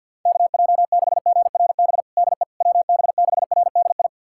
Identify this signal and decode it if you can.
Morse code, heard as a keyed tone, about 49 wpm, 700 Hz. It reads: KJ6GRB DE WB6RDI
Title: USA Callsign DE Callsign - 6 Characters - 50wpm